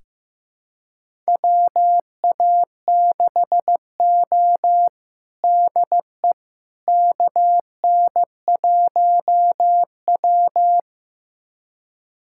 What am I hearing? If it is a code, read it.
Morse code, heard as a keyed tone, about 16 wpm, 700 Hz. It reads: WA6O DE KN1W
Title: USA Callsign DE Callsign - 4 Characters - 15wpm